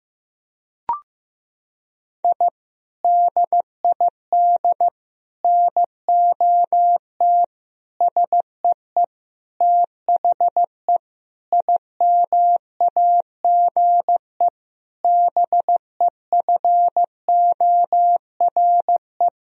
Morse code, heard as a keyed tone, about 15 wpm, 700 Hz. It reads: I DID NOT SEE THE IMAGE BEFORE
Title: Sentences from Top 1000 Words - Review Part 4 - 15wpm